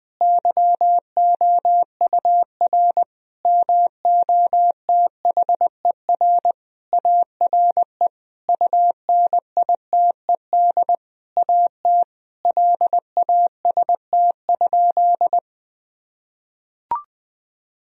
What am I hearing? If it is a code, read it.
Morse code, heard as a keyed tone, about 20 wpm, 700 Hz. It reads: YOUR MOTHER ARE UNITED AT LAST?
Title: Sentences from Top 1000 Words - Review Part 4 - 20wpm